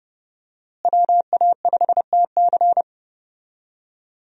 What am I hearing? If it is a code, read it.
Morse code, heard as a keyed tone, about 28 wpm, 700 Hz. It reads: WA5TC
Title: USA Call Signs - 5 Characters - Repeated 3x - 30wpm